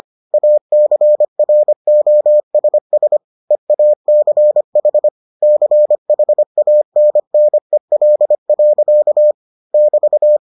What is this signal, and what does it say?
Morse code, heard as a keyed tone, about 25 wpm, 600 Hz. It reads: ACROSS EACH CHANNEL. <BT>